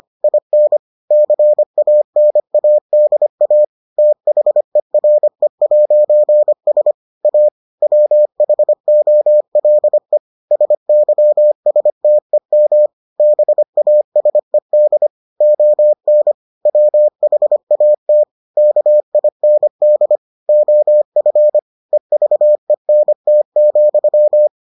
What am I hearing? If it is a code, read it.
Morse code, heard as a keyed tone, about 25 wpm, 600 Hz. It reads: IN CANADA THERE'S A WHOLE SYSTEM BASED ON WHAT KIND OF EVENT,